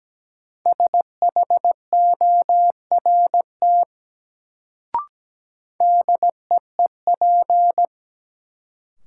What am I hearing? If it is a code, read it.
Morse code, heard as a keyed tone, about 17 wpm, 700 Hz. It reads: SHORT DEEP
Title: Sets of 5 Words Encore - New Word Drill - 17wpm